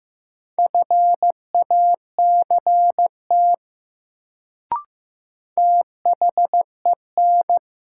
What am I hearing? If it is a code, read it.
Morse code, heard as a keyed tone, about 15 wpm, 700 Hz. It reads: FACT THEN
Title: Sets of 2 Words - New Word Drill - 15wpm